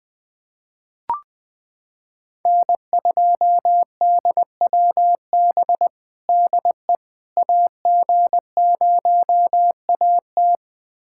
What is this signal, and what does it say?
Morse code, heard as a keyed tone, about 20 wpm, 700 Hz. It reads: N2DWB DE AG0AT
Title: USA Callsign DE Callsign - 5 Characters - 20wpm